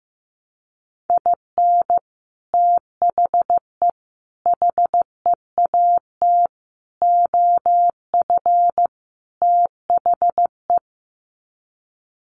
Morse code, heard as a keyed tone, about 15 wpm, 700 Hz. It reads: IN THE HEAT OF THE
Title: Sets of 5 Words Encore2 15wpm